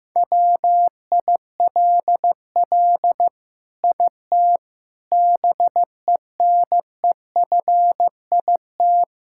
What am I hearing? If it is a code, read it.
Morse code, heard as a keyed tone, about 16 wpm, 700 Hz. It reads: WILL IT BENEFIT
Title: Sentences from Top 1000 Words - Review Part 6 - 15wpm